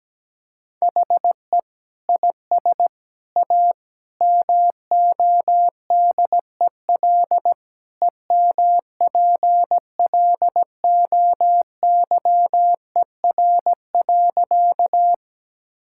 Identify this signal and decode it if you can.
Morse code, heard as a keyed tone, about 17 wpm, 700 Hz. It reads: HE IS A MODEL EMPLOYER.
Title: Sentences from Top 1600 Words 17wpm